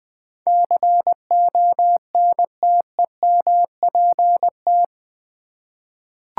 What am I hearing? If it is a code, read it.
Morse code, heard as a keyed tone, about 21 wpm, 700 Hz. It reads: CONTEMPT
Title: Eight Letter Words - 20wpm